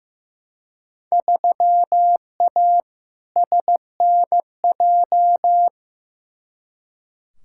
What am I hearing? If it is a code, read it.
Morse code, heard as a keyed tone, about 16 wpm, 700 Hz. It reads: 3A SNJ
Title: Field Day Exchanges - 15wpm